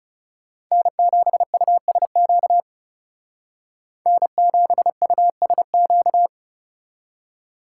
Morse code, heard as a keyed tone, about 38 wpm, 700 Hz. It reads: N7USQ N7USQ